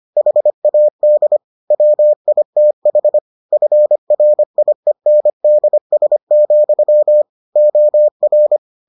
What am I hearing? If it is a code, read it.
Morse code, heard as a keyed tone, about 25 wpm, 600 Hz. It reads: HAD WITH FRIENDS, OR